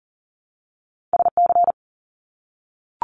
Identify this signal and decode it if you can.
Morse code, heard as a keyed tone, about 40 wpm, 700 Hz. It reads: SC
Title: US States Repeated 3x 40wpm